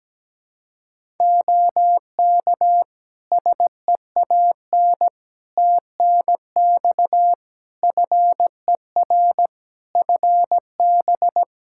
Morse code, heard as a keyed tone, about 17 wpm, 700 Hz. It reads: OK SEAN TNX FER FB
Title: Practice QSO - Exchange 4 - Ending - 17wpm